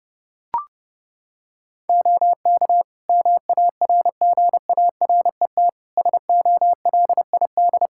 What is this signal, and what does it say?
Morse code, heard as a keyed tone, about 30 wpm, 700 Hz. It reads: OK MARGARET SOLID
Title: Practice QSO - Exchange 3 - More Info - 30wpm